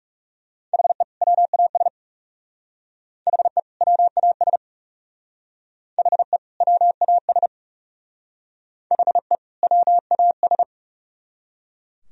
Morse code, heard as a keyed tone, about 36 wpm, 700 Hz. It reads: HE WAS HE WAS HE WAS HE WAS
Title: Speed-Racing - Sets of 2 Words 30wpm